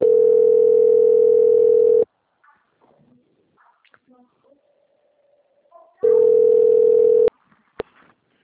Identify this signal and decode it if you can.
Ringback tone (North America), 440 + 480 Hz.
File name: out- - - - - .40249